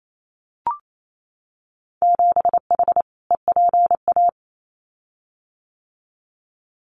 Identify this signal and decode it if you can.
Morse code, heard as a keyed tone, about 28 wpm, 690 Hz. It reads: 7H EPA
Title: Winter Field Day Exchanges Encore 28wpm